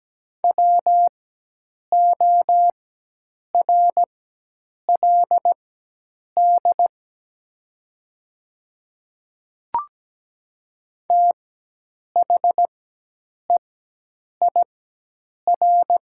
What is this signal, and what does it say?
Morse code, heard as a keyed tone, about 17 wpm, spaced as for 8 wpm, 700 Hz. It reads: WORLD THEIR
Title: 4x Character Spacing - Top 200 Words - 17wpm